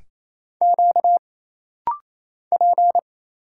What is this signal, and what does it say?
Morse code, heard as a keyed tone, about 27 wpm, 700 Hz. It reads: Q P
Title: Single Letters - Rapid-Fire 28wpm